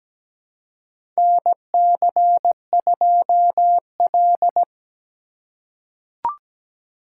Morse code, heard as a keyed tone, about 17 wpm, 700 Hz. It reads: NC2L
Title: USA Call Signs - 4 Characters - 3x Character Spacing - 17wpm